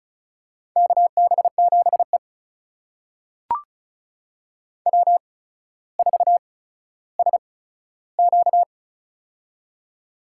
Morse code, heard as a keyed tone, about 36 wpm, spaced as for 15 wpm, 700 Hz. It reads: KB7E W4SQ
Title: USA Call Signs - 4 Characters - 8x Character Spacing - 35wpm